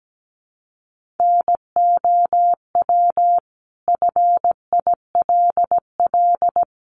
Morse code, heard as a keyed tone, about 17 wpm, 700 Hz. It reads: NOW FILL
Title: Sentences from Top 1500 Words 17wpm